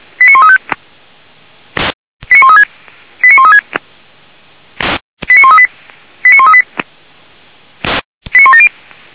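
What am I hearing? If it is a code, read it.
Five-tone selective call (ZVEI), heard as quick runs of five tones, tones of about 70 ms. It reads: 89135 89136 89136 89138 89138 89170
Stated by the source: ZVEI five-tone selective call received on an SDR.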